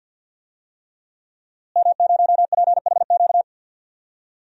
Morse code, heard as a keyed tone, about 49 wpm, 700 Hz. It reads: M0PHQ
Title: DXCC-100 Call Signs - 5 Characters - Repeated 3x - 50wpm